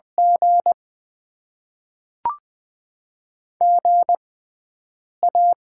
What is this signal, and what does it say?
Morse code, heard as a keyed tone, about 20 wpm, spaced as for 4 wpm, 700 Hz. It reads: GGA
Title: Top 20 QSO Elements - 6x Character Spacing - 20wpm